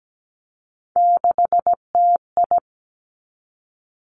Morse code, heard as a keyed tone, about 17 wpm, 700 Hz. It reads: 6TI
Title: Three Challenging Letter-Number Combinations 17wpm